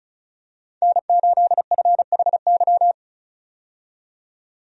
Morse code, heard as a keyed tone, about 34 wpm, 700 Hz. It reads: N8FHY
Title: USA Call Signs - 5 Characters - 35wpm